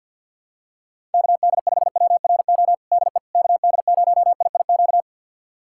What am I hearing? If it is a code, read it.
Morse code, heard as a keyed tone, about 45 wpm, 700 Hz. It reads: KD5WRO DE KD0IIQ
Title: USA Callsign DE Callsign - 6 Characters - 50wpm